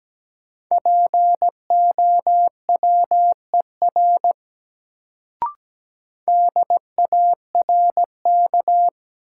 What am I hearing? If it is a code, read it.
Morse code, heard as a keyed tone, about 17 wpm, 700 Hz. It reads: POWER DARK
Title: Sentences from Top 500 Words - New Word Drill - 17wpm